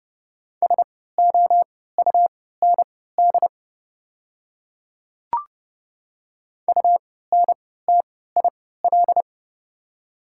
Morse code, heard as a keyed tone, about 32 wpm, spaced as for 17 wpm, 700 Hz. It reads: SOUND UNTIL
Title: 3x Character Spacing - Top 300 Words - 30wpm